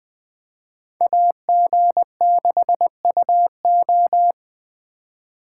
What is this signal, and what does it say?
Morse code, heard as a keyed tone, about 20 wpm, 700 Hz. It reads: AG6UO
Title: Speed Racing - USA 2x2 Call Signs 17wpm